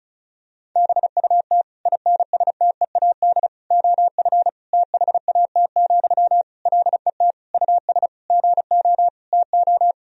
Morse code, heard as a keyed tone, about 36 wpm, 700 Hz. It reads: BUT INSTEAD OF THAT, LET US GO TO
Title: Sentences from Top 1000 Words - Review Part 4 - 35wpm